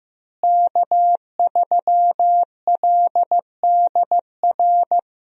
Morse code, heard as a keyed tone, about 15 wpm, 700 Hz. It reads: K3LDR
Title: USA Call Signs - 5 Characters - Repeated 3x - 15wpm